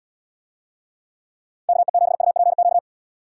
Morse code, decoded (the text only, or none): KE7UPQ